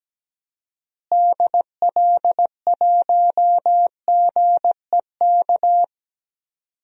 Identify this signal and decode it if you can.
Morse code, heard as a keyed tone, about 17 wpm, 700 Hz. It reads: DL1GEK
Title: DXCC-100 Call Signs - 6 Characters - Repeated 2x - 17wpm